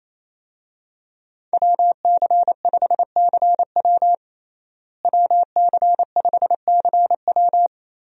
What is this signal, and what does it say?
Morse code, heard as a keyed tone, about 28 wpm, 700 Hz. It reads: WC5CW WC5CW
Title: USA Call Signs - 5 Characters - Repeated 3x - 28wpm